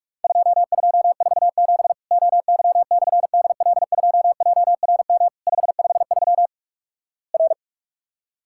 Morse code, heard as a keyed tone, about 44 wpm, 700 Hz. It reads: 2247 OY/DL2JRM 553 R
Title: Worked All Europe DX Contest - QTC Exchange - 45wpm